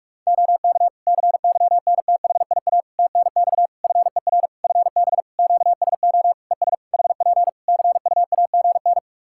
Morse code, heard as a keyed tone, about 46 wpm, 700 Hz. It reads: OK CYNTHIA TNX FER FB QSO ES HP CUAGN
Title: K1USN SST QSO Practice 45wpm